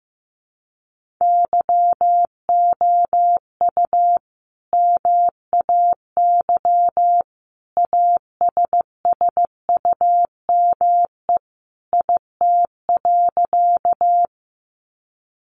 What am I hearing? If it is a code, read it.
Morse code, heard as a keyed tone, about 15 wpm, 700 Hz. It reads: YOU MAY ASSUME IT.
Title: Sentences from Top 1500 Words 15wpm